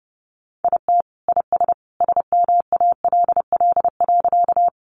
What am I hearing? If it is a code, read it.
Morse code, heard as a keyed tone, about 30 wpm, 700 Hz. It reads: IT IS SMALL.
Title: Additional Easy Phrases 30wpm